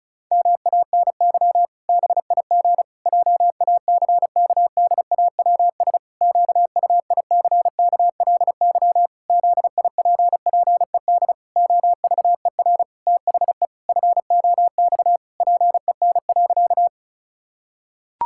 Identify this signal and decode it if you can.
Morse code, heard as a keyed tone, about 36 wpm, 700 Hz. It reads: MANY BIG JACKDAWS QUICKLY ZIPPED OVER THE FOX PEN.
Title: Pangram Challenge v2 - 35wpm